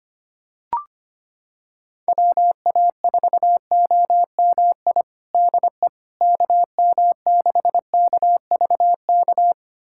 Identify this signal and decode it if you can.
Morse code, heard as a keyed tone, about 25 wpm, 700 Hz. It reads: WA4OMI DE KM6KVK
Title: USA Callsign DE Callsign - 6 Characters - 25wpm